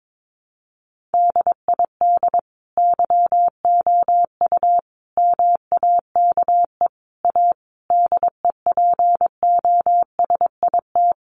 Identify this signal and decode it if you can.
Morse code, heard as a keyed tone, about 22 wpm, 700 Hz. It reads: DID YOU MAKE A DEPOSIT
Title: Sentences from Top 1300 Words 22wpm